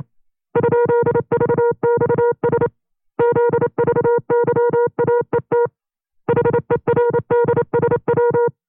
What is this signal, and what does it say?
Morse code, heard as a keyed tone, about 28 wpm, 480 Hz. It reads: ?VXS ZVYAET HERDSW
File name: FreeFiles / CW-practice / 25WPM-side1